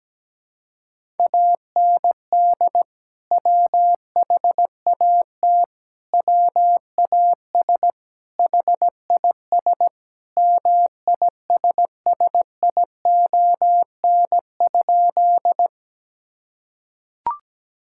Morse code, heard as a keyed tone, about 17 wpm, 700 Hz. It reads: AND WHAT WAS HIS MISSION?
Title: Sentences from Top 1600 Words 17wpm